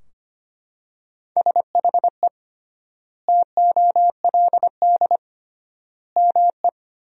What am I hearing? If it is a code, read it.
Morse code, heard as a keyed tone, about 25 wpm, 700 Hz. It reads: SHE TOLD ME
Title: 3x Word-Spacing - Sentences from Top 400 Words - 25wpm